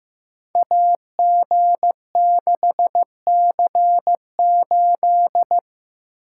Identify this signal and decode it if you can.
Morse code, heard as a keyed tone, about 15 wpm, 700 Hz. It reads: AG6C8
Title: Code Groups - 5 Letter:Numbers - 15wpm